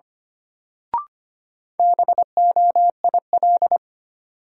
Morse code, heard as a keyed tone, about 25 wpm, 700 Hz. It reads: BOIL
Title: Sets of 5 Words Encore - New Word Drill - 25wpm